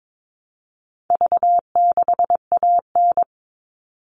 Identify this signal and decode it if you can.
Morse code, heard as a keyed tone, about 22 wpm, 700 Hz. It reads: V6AN
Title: All Cut Numbers - 4-Digit 22wpm